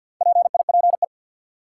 Morse code, heard as a keyed tone, about 49 wpm, 700 Hz. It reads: PIPE
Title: Sentences from Top 1500 Words - New Word Drill - 50wpm